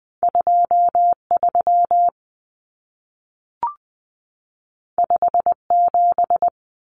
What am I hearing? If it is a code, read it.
Morse code, heard as a keyed tone, about 20 wpm, 700 Hz. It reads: 23 57
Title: Common Cut Numbers - 2-Digit 20wpm